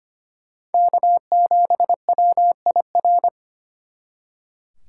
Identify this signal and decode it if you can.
Morse code, heard as a keyed tone, about 33 wpm, 700 Hz. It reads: K7WIR